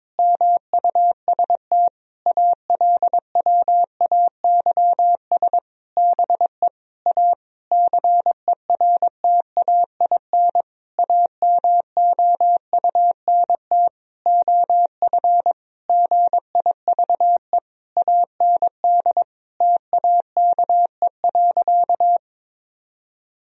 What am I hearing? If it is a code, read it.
Morse code, heard as a keyed tone, about 22 wpm, 700 Hz. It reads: MUST ALWAYS BE A CERTAIN AMOUNT OF GIVE AND TAKE.
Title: Sentences from Top 1000 Words - Review Part 4 - 22wpm